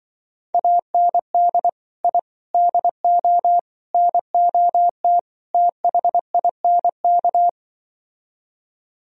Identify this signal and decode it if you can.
Morse code, heard as a keyed tone, about 24 wpm, 700 Hz. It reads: AND I DO NOT THINK
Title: Speed-Racing - Sets of 5 Words 20wpm